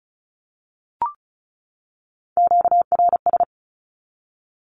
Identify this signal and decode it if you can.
Morse code, heard as a keyed tone, about 34 wpm, 700 Hz. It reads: QRS
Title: Q-Codes 35wpm